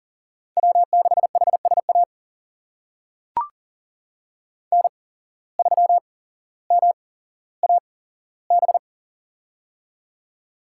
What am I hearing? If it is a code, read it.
Morse code, heard as a keyed tone, about 40 wpm, spaced as for 17 wpm, 700 Hz. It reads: W6HSA N3MAB
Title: USA Call Signs - 5 Characters - 8x Character Spacing - 40wpm